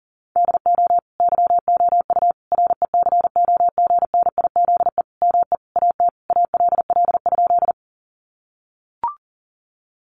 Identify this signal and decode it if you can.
Morse code, heard as a keyed tone, about 40 wpm, 700 Hz. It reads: DO YOU RECOGNIZE ME AT ALL?
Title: Sentences from Top 1400 Words 40wpm